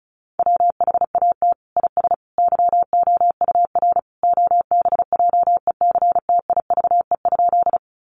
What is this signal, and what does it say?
Morse code, heard as a keyed tone, about 34 wpm, 700 Hz. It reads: WHAT IS YOUR OBJECTIVE?
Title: Sentences from Top 1200 Words 35wpm